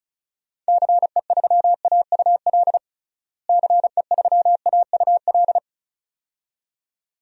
Morse code, heard as a keyed tone, about 34 wpm, 700 Hz. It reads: CE3AUL CE3AUL
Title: DXCC-100 Call Signs - 6 Characters - Repeated 2x - 35wpm